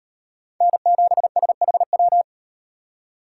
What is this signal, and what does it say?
Morse code, heard as a keyed tone, about 38 wpm, 700 Hz. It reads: N7SHW